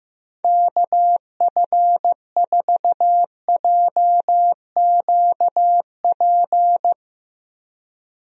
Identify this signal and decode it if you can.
Morse code, heard as a keyed tone, about 15 wpm, 700 Hz. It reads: KF4JQP
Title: USA Call Signs - 6 Characters - Repeated 2x - 15wpm